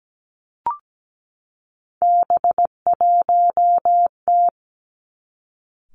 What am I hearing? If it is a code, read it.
Morse code, heard as a keyed tone, about 17 wpm, 700 Hz. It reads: B1T
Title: Three Challenging Letter-Number Combinations 17wpm